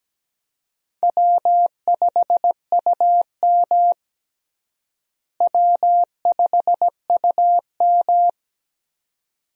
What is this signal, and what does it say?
Morse code, heard as a keyed tone, about 17 wpm, 700 Hz. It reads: W5UM W5UM
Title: USA Call Signs - 4 Characters - Repeated 2x - 17wpm